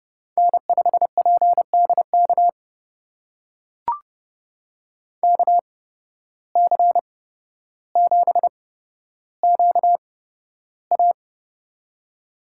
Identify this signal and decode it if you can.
Morse code, heard as a keyed tone, about 30 wpm, spaced as for 13 wpm, 700 Hz. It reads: N5PDK KC7QA
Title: USA Call Signs - 5 Characters - 8x Character Spacing - 30wpm